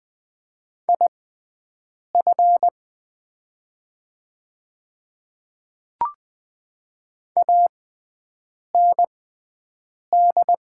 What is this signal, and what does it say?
Morse code, heard as a keyed tone, about 20 wpm, spaced as for 7 wpm, 700 Hz. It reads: IF AND
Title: 6x Character Spacing - Top 100 Words - 20wpm